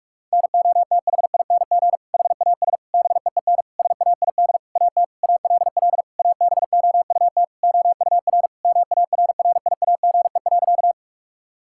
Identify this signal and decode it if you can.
Morse code, heard as a keyed tone, about 41 wpm, 700 Hz. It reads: NOTHING HAS BEEN SAID AT ALL ABOUT OUR MARRIAGE.
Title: Sentences from Top 1000 Words - Review Part 5 - 45wpm